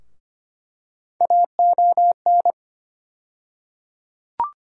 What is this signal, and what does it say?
Morse code, heard as a keyed tone, about 25 wpm, 700 Hz. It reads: AON
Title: USA Call Sign Postfixes - 3 Letters 25wpm